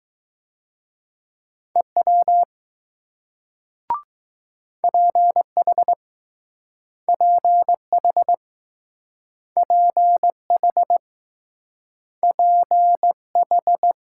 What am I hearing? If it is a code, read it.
Morse code, heard as a keyed tone, about 21 wpm, 700 Hz. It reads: EW PH PH PH PH